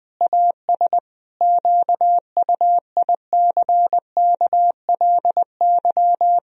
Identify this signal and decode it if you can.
Morse code, heard as a keyed tone, about 20 wpm, 700 Hz. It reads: AS QUICKLY
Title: Sentences from Top 1000 Words - Review Part 3 - 20wpm